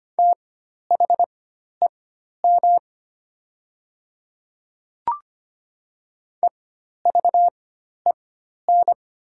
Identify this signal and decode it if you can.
Morse code, heard as a keyed tone, about 25 wpm, spaced as for 12 wpm, 700 Hz. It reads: THEM EVEN
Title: 4x Character Spacing - Top 200 Words - 25wpm